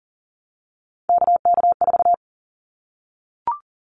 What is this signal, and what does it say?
Morse code, heard as a keyed tone, about 42 wpm, 700 Hz. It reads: KK4
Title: Callsign Prefixes for Top 25 DXCC - 40wpm